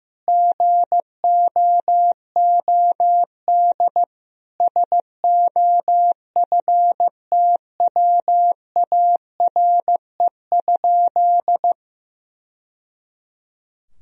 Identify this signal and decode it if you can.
Morse code, heard as a keyed tone, about 15 wpm, 700 Hz. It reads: GOOD SOFTWARE?
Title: Sentences from Top 1000 Words - Review Part 3 - 15wpm